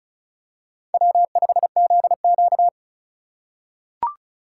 Morse code, heard as a keyed tone, about 34 wpm, 700 Hz. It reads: W5ZQ
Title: USA Call Signs - 4 Characters - 3x Character Spacing - 35wpm